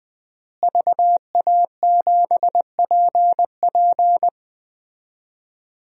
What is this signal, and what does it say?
Morse code, heard as a keyed tone, about 20 wpm, 700 Hz. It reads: VA7PP
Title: DXCC-100 Call Signs - 5 Characters - Repeated 3x - 20wpm